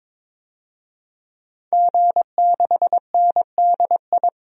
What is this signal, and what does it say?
Morse code, heard as a keyed tone, about 22 wpm, 700 Hz. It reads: G6NDI
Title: DXCC-100 Call Signs - 5 Characters - 22wpm